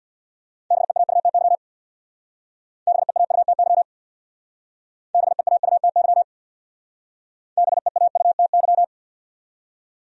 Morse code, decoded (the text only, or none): BEAUTY BEAUTY BEAUTY BEAUTY